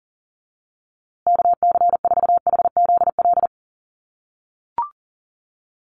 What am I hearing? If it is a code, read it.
Morse code, heard as a keyed tone, about 40 wpm, 700 Hz. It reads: KC4HZL
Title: USA Call Signs Repeated 3x 40wpm